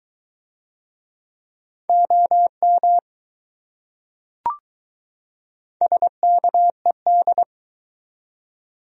Morse code, heard as a keyed tone, about 20 wpm, 700 Hz. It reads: OM SKED